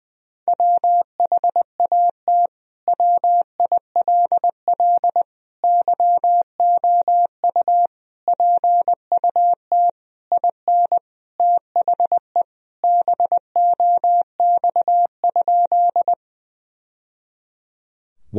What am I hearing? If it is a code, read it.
Morse code, heard as a keyed tone, about 20 wpm, 700 Hz. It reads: WHAT WILL YOU PUT IN THE BOX?
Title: Sentences from Top 1000 Words - Review Part 2 - 20wpm